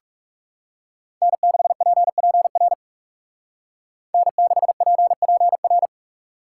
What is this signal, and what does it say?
Morse code, heard as a keyed tone, about 39 wpm, 700 Hz. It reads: N6PPR N6PPR